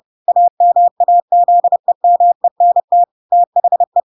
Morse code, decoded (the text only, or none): AMAZEMENT THE